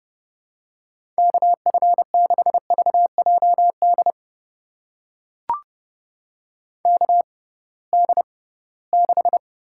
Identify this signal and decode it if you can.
Morse code, heard as a keyed tone, about 30 wpm, spaced as for 20 wpm, 700 Hz. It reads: KF6VJD KD6
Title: USA Call Signs - 6 Characters - 6x Character Spacing - 30wpm